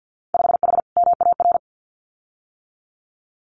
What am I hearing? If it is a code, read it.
Morse code, heard as a keyed tone, about 50 wpm, 700 Hz. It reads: 5H MAR
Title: Winter Field Day Exchanges Encore 50wpm